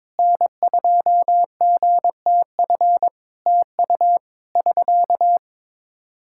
Morse code, heard as a keyed tone, about 22 wpm, 700 Hz. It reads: N2GTF TU <SK>
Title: Practice QSO - Exchange 4 - Ending - 22wpm